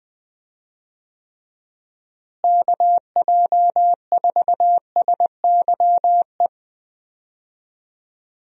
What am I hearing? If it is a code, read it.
Morse code, heard as a keyed tone, about 21 wpm, 700 Hz. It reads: KJ4SYE
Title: USA Call Signs - 6 Characters - Repeated 2x - 20wpm